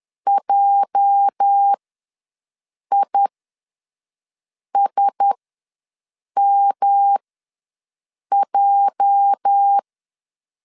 Morse code, heard as a keyed tone, about 10 wpm, spaced as for 5 wpm, 800 Hz. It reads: JISMJ
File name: LEZIONI / 8 YJ45 / 8 YJ ws 10wpm 4043